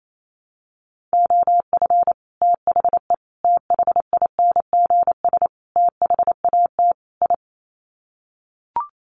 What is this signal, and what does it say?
Morse code, heard as a keyed tone, about 28 wpm, 700 Hz. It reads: OF THE THINGS THAT I
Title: Sets of 5 Words Encore 28wpm